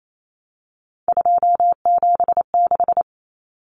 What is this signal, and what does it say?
Morse code, heard as a keyed tone, about 28 wpm, 700 Hz. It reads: 276
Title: Numbers - 3-Digit 28wpm